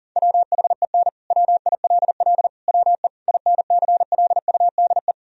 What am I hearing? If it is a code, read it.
Morse code, heard as a keyed tone, about 37 wpm, 700 Hz. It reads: WHEN WILL WE INCLUDE
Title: Sentences from Top 1000 Words - Review Part 3 - 40wpm